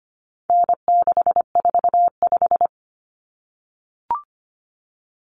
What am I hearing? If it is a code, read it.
Morse code, heard as a keyed tone, about 25 wpm, 700 Hz. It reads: N645
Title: Common Cut Numbers - 4-Digit 25wpm